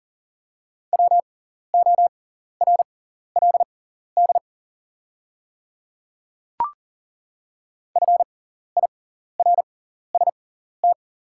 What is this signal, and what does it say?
Morse code, heard as a keyed tone, about 40 wpm, spaced as for 14 wpm, 690 Hz. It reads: WORLD FIRST
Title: 6x Character Spacing - Top 200 Words - 40wpm